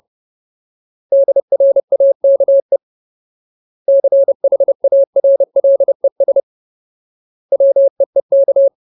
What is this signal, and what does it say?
Morse code, heard as a keyed tone, about 30 wpm, 550 Hz. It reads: DRAKE CHARLES WEEK
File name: Ultimate Word List / 30WPM Ultimate